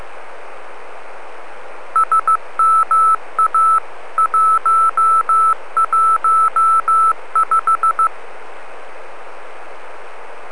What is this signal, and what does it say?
Morse code, heard as a keyed tone, about 15 wpm, 1320 Hz. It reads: SMA 115